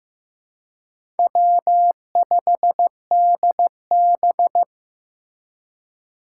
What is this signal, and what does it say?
Morse code, heard as a keyed tone, about 15 wpm, 700 Hz. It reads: W5DB
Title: USA Call Signs - 4 Characters - 4x Character Spacing - 15wpm